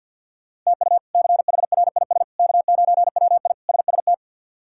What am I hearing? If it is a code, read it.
Morse code, heard as a keyed tone, about 51 wpm, 700 Hz. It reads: TU CHRIS K9WI SST
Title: K1USN SST QSO Practice - 50wpm